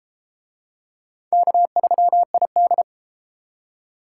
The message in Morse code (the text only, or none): K3ID